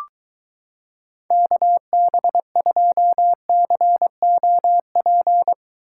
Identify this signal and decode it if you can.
Morse code, heard as a keyed tone, about 23 wpm, 700 Hz. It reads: KB2COP